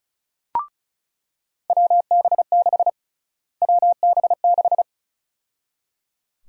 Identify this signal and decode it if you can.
Morse code, heard as a keyed tone, about 34 wpm, 700 Hz. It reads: WB6 WB6
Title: USA Call Sign Prefixes - Repeated 2x - 35wpm